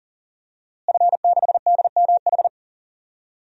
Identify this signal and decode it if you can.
Morse code, heard as a keyed tone, about 38 wpm, 700 Hz. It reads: F6DMH
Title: DXCC-100 Call Signs - 5 Characters - Repeated 3x - 40wpm